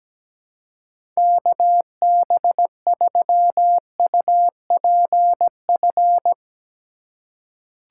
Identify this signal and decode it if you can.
Morse code, heard as a keyed tone, about 17 wpm, 700 Hz. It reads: KB3UPF
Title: USA Call Signs - 6 Characters - 2x Character Spacing - 17wpm